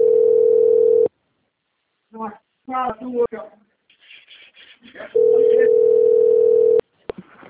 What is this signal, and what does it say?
Ringback tone (North America), 440 + 480 Hz, with speech.
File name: out- - - - - .17114